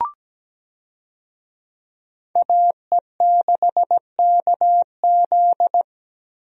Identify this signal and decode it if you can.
Morse code, heard as a keyed tone, about 17 wpm, 700 Hz. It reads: AE6KZ
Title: USA Call Signs - 5 Characters - Repeated 2x - 17wpm